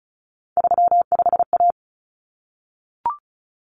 Morse code, heard as a keyed tone, about 34 wpm, 700 Hz. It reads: 35A
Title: Common Cut Numbers - 3-Digit 35wpm